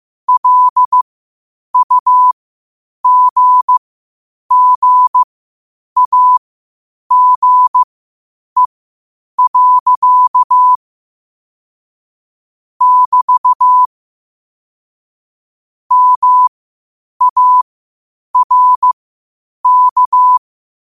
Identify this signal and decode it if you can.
Morse code, heard as a keyed tone, about 15 wpm, spaced as for 8 wpm, 1000 Hz. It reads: LUGGAGE. <BT> MARK